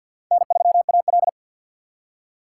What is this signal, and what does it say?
Morse code, heard as a keyed tone, about 50 wpm, 700 Hz. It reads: N3AL
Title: USA Call Signs - 4x Character Spacing - 50wpm